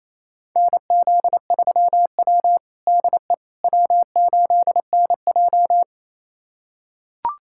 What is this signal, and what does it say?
Morse code, heard as a keyed tone, about 28 wpm, 700 Hz. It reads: NZ3W DE W8NJ
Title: USA Callsign DE Callsign - 4 Characters - 28wpm